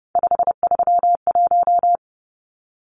Morse code, heard as a keyed tone, about 30 wpm, 700 Hz. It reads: 531
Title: Numbers - 3-Digit 30wpm